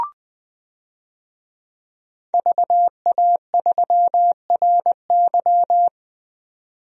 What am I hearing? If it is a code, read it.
Morse code, heard as a keyed tone, about 20 wpm, 700 Hz. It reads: VA3RY
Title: DXCC-100 Call Signs - 5 Characters - 20wpm